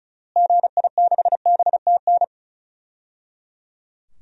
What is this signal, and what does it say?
Morse code, heard as a keyed tone, about 36 wpm, 700 Hz. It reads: GI6BTN
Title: DXCC-100 Call Signs - 6 Characters - Repeated 3x - 35wpm